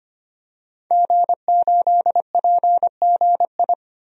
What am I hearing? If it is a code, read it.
Morse code, heard as a keyed tone, about 25 wpm, 700 Hz. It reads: G8PGI
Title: DXCC-100 Call Signs - 5 Characters - 25wpm